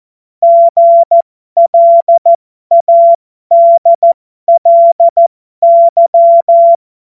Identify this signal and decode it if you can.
Morse code, heard as a keyed tone, about 14 wpm, spaced as for 12 wpm, 680 Hz. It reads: GLADLY
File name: Morsekurs / Witze